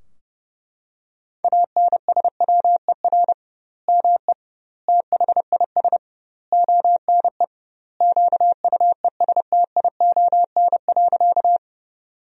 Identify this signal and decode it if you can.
Morse code, heard as a keyed tone, about 30 wpm, 700 Hz. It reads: ANSWER ME THIS ONE QUESTION.
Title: 2x Word-Spacing - Sentences from Top 400 Words - 30wpm